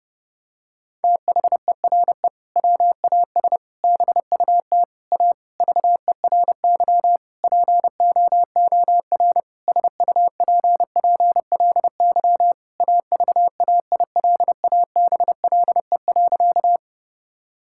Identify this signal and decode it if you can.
Morse code, heard as a keyed tone, about 32 wpm, 700 Hz. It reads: THERE WAS BUT A VERY POOR SUPPLY AVAILABLE.
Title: Sentences from Top 1000 Words - Review Part 5 - 30wpm